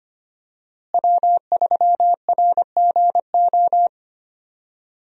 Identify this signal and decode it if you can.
Morse code, heard as a keyed tone, about 25 wpm, 700 Hz. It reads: W3RGO
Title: USA Call Signs - 5 Characters - Repeated 3x - 25wpm